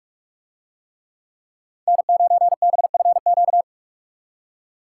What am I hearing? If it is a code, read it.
Morse code, heard as a keyed tone, about 45 wpm, 700 Hz. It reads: N9BFQ